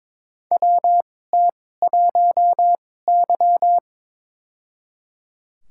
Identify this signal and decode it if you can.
Morse code, heard as a keyed tone, about 22 wpm, spaced as for 16 wpm, 700 Hz. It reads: WT1Y
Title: USA Call Signs - 4 Characters - 2x Character Spacing - 22wpm